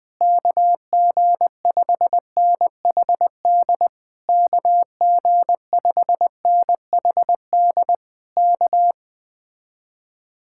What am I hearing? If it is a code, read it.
Morse code, heard as a keyed tone, about 20 wpm, 700 Hz. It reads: KG5NHD KG5NHD K
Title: Speed-Racing Common QSO Phrases - 17wpm